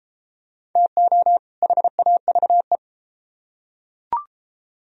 Morse code, heard as a keyed tone, about 31 wpm, 700 Hz. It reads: TO HAVE